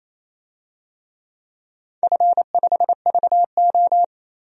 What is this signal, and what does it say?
Morse code, heard as a keyed tone, about 28 wpm, 700 Hz. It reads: F5VO
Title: DXCC-100 Call Signs - 4 Characters - 28wpm